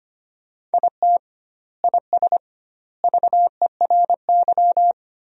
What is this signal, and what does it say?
Morse code, heard as a keyed tone, about 25 wpm, 700 Hz. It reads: IT IS VERY
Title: 2x Word-Spacing - Sentences from Top 200 Words - 25wpm